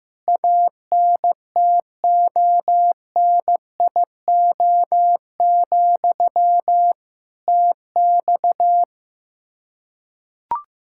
Morse code, heard as a keyed tone, about 15 wpm, 700 Hz. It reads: ANTONIO, TX
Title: K1USN SST QSO Practice 15wpm